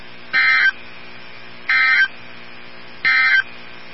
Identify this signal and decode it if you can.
Emergency Alert System (SAME) tones, heard as harsh buzzing bursts. It is heard from 0.2 to 3.5 s.
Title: NOAA Weather Radio 162.550